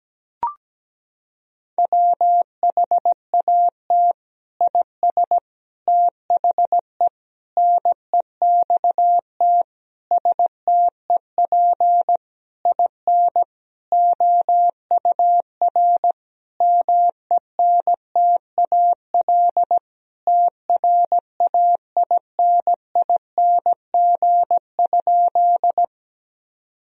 Morse code, heard as a keyed tone, about 17 wpm, 700 Hz. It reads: WHAT IS THE NEXT STEP IN OUR MENTAL TRAINING?
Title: Sentences from Top 1000 Words - Review Part 4 - 17wpm